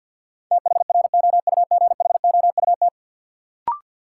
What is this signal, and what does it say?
Morse code, heard as a keyed tone, about 51 wpm, 700 Hz. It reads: THROUGHOUT
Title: Sentences from Top 800 Words - New Word Drill - 50wpm